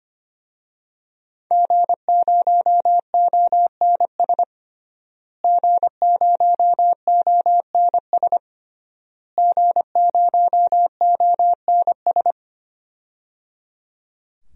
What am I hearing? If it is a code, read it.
Morse code, heard as a keyed tone, about 25 wpm, 700 Hz. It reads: G0ONS G0ONS G0ONS
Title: DXCC-100 Call Signs - 5 Characters - Repeated 3x - 25wpm